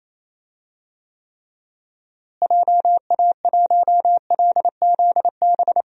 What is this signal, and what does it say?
Morse code, heard as a keyed tone, about 28 wpm, 700 Hz. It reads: JA1LZB
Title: DXCC-100 Call Signs - 6 Characters - Repeated 2x - 28wpm